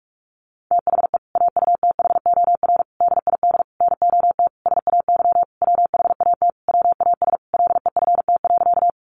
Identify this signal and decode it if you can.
Morse code, heard as a keyed tone, about 44 wpm, 700 Hz. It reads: THE AUTHOR DID NOT SAY WHAT WAS LEFT.
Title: Sentences from Top 1200 Words 45wpm